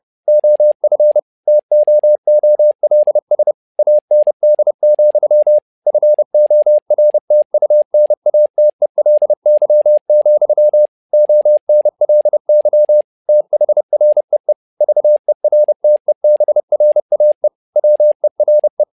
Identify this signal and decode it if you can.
Morse code, heard as a keyed tone, about 30 wpm, 600 Hz. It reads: OF TOOLS AND, FORTUNATELY, ONLY THREE VERTEBRAE WERE
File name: learning center / tower trouble / tower trouble 30wpm